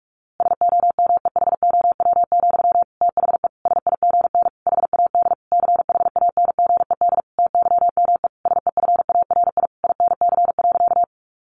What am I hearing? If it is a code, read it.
Morse code, heard as a keyed tone, about 44 wpm, 700 Hz. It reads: SOMEHOW, THE SIGN HAD CHANGED TYME SEFARI INC.
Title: A Sound Of Thunder 45wpm